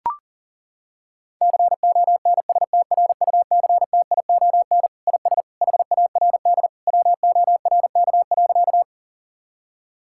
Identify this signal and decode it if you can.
Morse code, heard as a keyed tone, about 40 wpm, 700 Hz. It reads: CONSTRUCTION IS HARD WORK.
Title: Sentences from Top 1600 Words 40wpm